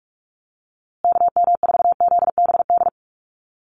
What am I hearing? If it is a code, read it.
Morse code, heard as a keyed tone, about 44 wpm, 700 Hz. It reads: KM4ZBD
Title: USA Call Signs Repeated 3x 45wpm